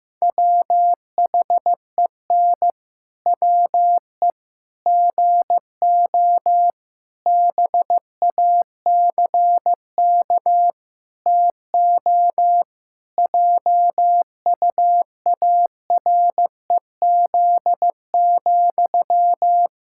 Morse code, heard as a keyed tone, about 16 wpm, 700 Hz. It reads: WHEN WE GO BACK TO JUAREZ,
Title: Pangram Challenge v2 - 15wpm